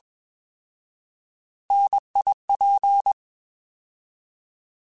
Morse code, heard as a keyed tone, about 21 wpm, 800 Hz. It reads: NIP
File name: Alphabetical / Alphabetical mp3s / Ns @20